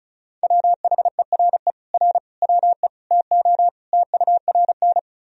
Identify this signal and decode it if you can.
Morse code, heard as a keyed tone, about 34 wpm, 700 Hz. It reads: WHERE R WE TO TURN
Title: Sentences from Top 200 Words and Abbreviations 35wpm